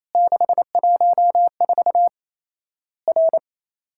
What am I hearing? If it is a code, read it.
Morse code, heard as a keyed tone, about 28 wpm, 700 Hz. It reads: 614 R
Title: Worked All Europe DX Contest - QTC Exchange - 28wpm